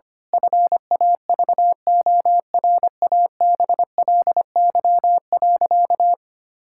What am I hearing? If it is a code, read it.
Morse code, heard as a keyed tone, about 25 wpm, 700 Hz. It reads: FAVORABLY.
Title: Pangram Challenge v2 - 25wpm